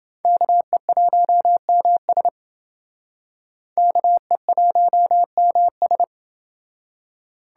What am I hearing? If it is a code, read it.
Morse code, heard as a keyed tone, about 29 wpm, 700 Hz. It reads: KE1MS KE1MS